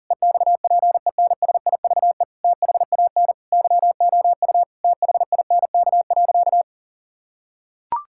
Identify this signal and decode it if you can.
Morse code, heard as a keyed tone, about 37 wpm, 700 Hz. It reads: EXPENSIVE THAN YOU THINK.
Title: Sentences from Top 1000 Words - Review Part 6 - 40wpm